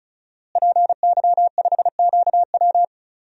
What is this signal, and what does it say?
Morse code, heard as a keyed tone, about 36 wpm, 700 Hz. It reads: PY5QW
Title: DXCC-100 Call Signs - 5 Characters - Repeated 3x - 35wpm